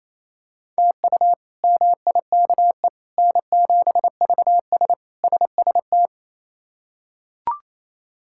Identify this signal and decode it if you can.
Morse code, heard as a keyed tone, about 28 wpm, 700 Hz. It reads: TU MIKE N7VS SST
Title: K1USN SST QSO Practice - 28wpm